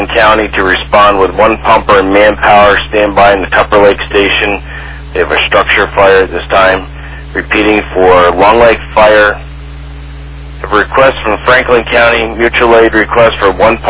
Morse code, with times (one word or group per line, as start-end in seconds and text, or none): none